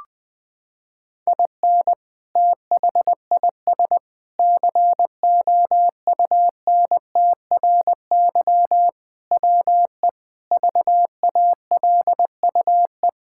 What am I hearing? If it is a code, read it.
Morse code, heard as a keyed tone, about 20 wpm, 700 Hz. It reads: IN THIS COUNTRY WE VALUE
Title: Sentences with 3-Word Phrases - 20wpm